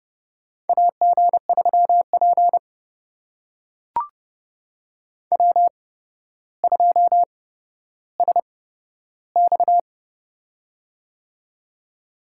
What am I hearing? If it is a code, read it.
Morse code, heard as a keyed tone, about 30 wpm, spaced as for 13 wpm, 700 Hz. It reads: AG3P W2SX
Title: USA Call Signs - 4 Characters - 8x Character Spacing - 30wpm